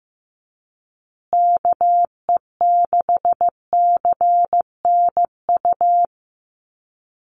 Morse code, heard as a keyed tone, about 15 wpm, 700 Hz. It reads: KE6CNU
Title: USA Call Signs Repeated 2x 15wpm